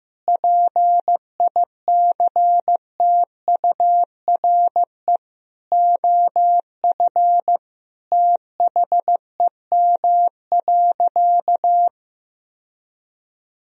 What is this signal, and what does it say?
Morse code, heard as a keyed tone, about 15 wpm, 700 Hz. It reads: PICTURE OF THEM.
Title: Sentences from Top 1000 Words - Review Part 1 - 15wpm